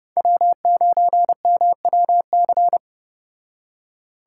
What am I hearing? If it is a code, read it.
Morse code, heard as a keyed tone, about 31 wpm, 700 Hz. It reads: W9MWC